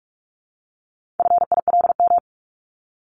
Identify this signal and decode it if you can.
Morse code, heard as a keyed tone, about 44 wpm, 700 Hz. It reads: FILM
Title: Top 501-750 Words 45wpm